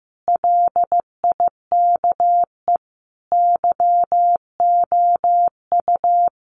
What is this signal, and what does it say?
Morse code, heard as a keyed tone, about 15 wpm, 700 Hz. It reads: LIKE YOU
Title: Sentences from Top 1400 Words 15wpm